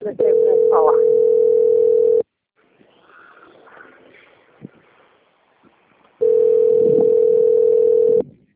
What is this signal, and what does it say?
Ringback tone (North America), 440 + 480 Hz, with speech.
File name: out- - - - - .70301